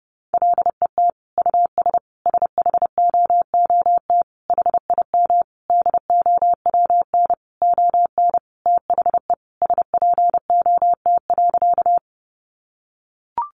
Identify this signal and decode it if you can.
Morse code, heard as a keyed tone, about 30 wpm, 700 Hz. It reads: LET US SHOOT HIM DOWN ON THE SPOT.
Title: Sentences from Top 1200 Words 30wpm